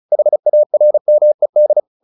Morse code, heard as a keyed tone, about 34 wpm, 600 Hz. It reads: HARMED
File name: the old man and the sea 2 35wpm